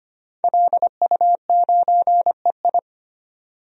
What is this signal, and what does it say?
Morse code, heard as a keyed tone, about 25 wpm, 700 Hz. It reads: LU9EI
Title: DXCC-100 Call Signs - 5 Characters - 25wpm